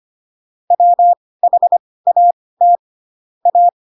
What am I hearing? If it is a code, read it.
Morse code, heard as a keyed tone, about 25 wpm, spaced as for 18 wpm, 700 Hz. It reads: WHAT A
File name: Pinocchio / Pinocchio ch1 12 25 ik1hge